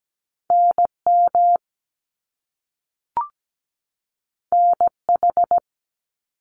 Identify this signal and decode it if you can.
Morse code, heard as a keyed tone, about 17 wpm, 700 Hz. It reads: NM NH
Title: US States 17wpm